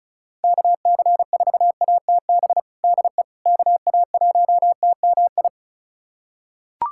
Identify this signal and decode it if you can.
Morse code, heard as a keyed tone, about 36 wpm, 700 Hz. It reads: KC4ATB DE KA1TMI
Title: USA Callsign DE Callsign - 6 Characters - 35wpm